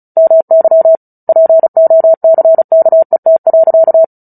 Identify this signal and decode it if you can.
Morse code, heard as a keyed tone, about 34 wpm, 650 Hz. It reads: MY POCKET.
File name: Book Two / 09-Chapter 9 - 35 WPM